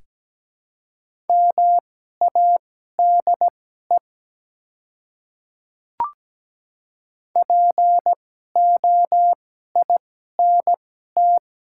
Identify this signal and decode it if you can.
Morse code, heard as a keyed tone, about 17 wpm, spaced as for 12 wpm, 700 Hz. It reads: MADE POINT
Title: 2x Character Spacing - Top 200 Words - 17wpm